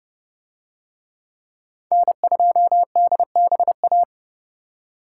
Morse code, heard as a keyed tone, about 26 wpm, 700 Hz. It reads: N2DBA